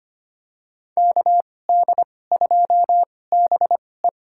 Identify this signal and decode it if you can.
Morse code, heard as a keyed tone, about 25 wpm, spaced as for 18 wpm, 700 Hz. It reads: KD2BE
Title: USA Call Signs - 5 Characters - 2x Character Spacing - 25wpm